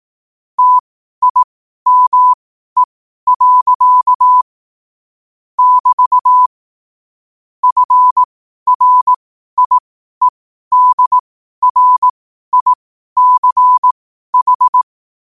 Morse code, heard as a keyed tone, about 18 wpm, spaced as for 12 wpm, 1000 Hz. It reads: TIME. <BT> FRIEDRICH
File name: qotdfiles / 12wpm